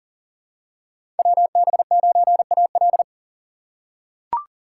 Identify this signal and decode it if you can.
Morse code, heard as a keyed tone, about 40 wpm, 700 Hz. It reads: WB9AL
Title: USA Call Signs - 6x Character Spacing - 40wpm